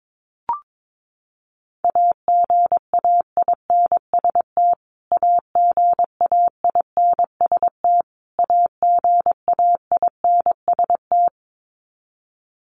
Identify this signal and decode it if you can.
Morse code, heard as a keyed tone, about 22 wpm, 700 Hz. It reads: AGAINST AGAINST AGAINST
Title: Top 500 Words Repeated 22wpm